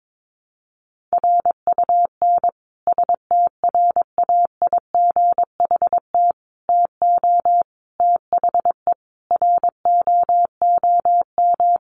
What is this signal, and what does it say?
Morse code, heard as a keyed tone, about 22 wpm, 700 Hz. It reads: RUN STRAIGHT TO THE ROOM
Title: Sentences from Top 1000 Words 22wpm